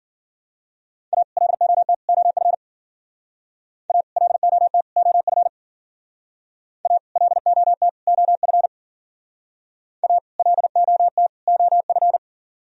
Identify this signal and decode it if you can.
Morse code, heard as a keyed tone, about 51 wpm, 700 Hz. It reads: A LOT OF A LOT OF A LOT OF A LOT OF